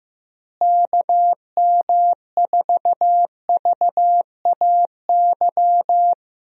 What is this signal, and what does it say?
Morse code, heard as a keyed tone, about 15 wpm, 700 Hz. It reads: KM4VAY
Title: USA Call Signs - 6 Characters - Repeated 2x - 15wpm